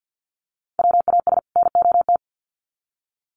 Morse code, heard as a keyed tone, about 50 wpm, 700 Hz. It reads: WAS NOT
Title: Sets of 2 Words 50wpm